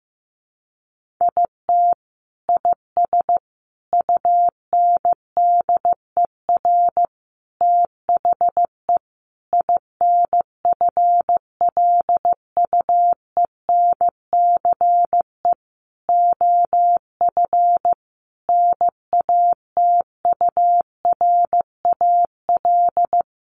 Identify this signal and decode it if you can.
Morse code, heard as a keyed tone, about 15 wpm, 700 Hz. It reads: IT IS UNDER THE INFLUENCE OF NATURAL
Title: Sentences from Top 1400 Words 15wpm